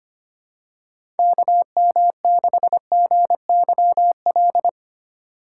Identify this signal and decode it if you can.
Morse code, heard as a keyed tone, about 25 wpm, 700 Hz. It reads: KM6GYL
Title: USA Call Signs - 4x Character Spacing - 25wpm